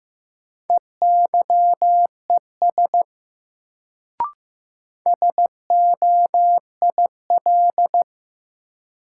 Morse code, heard as a keyed tone, about 15 wpm, 700 Hz. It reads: EYES SOIL
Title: Sentences from Top 1000 Words - Review Part 4 - New Word Drill - 15wpm